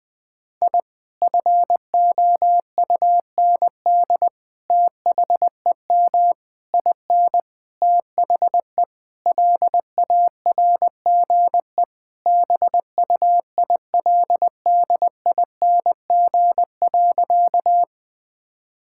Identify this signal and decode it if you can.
Morse code, heard as a keyed tone, about 21 wpm, 700 Hz. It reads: I FOUND THEM IN THE LARGE BUILDING.
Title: Sentences from Top 1000 Words - Review Part 5 - 20wpm